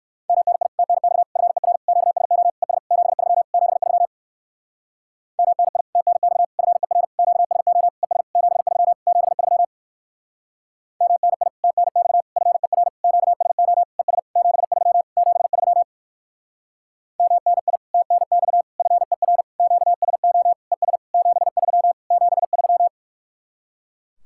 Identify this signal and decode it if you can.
Morse code, heard as a keyed tone, about 59 wpm, 700 Hz. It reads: MNI TNX FER QSO ES 73 73 MNI TNX FER QSO ES 73 73 MNI TNX FER QSO ES 73 73 MNI TNX FER QSO ES 73 73